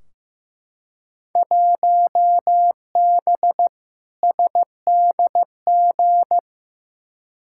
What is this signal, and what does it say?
Morse code, heard as a keyed tone, about 15 wpm, 700 Hz. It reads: 1B SDG
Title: Field Day Exchanges - 15wpm